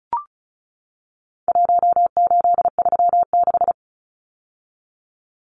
Morse code, heard as a keyed tone, about 34 wpm, 700 Hz. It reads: 1836
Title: Numbers - Contest Style - 4-Digit 35wpm